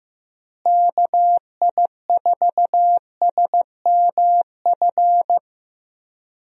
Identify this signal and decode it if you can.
Morse code, heard as a keyed tone, about 15 wpm, 700 Hz. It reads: KI4SMF
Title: USA Call Signs - 6 Characters - Repeated 3x - 15wpm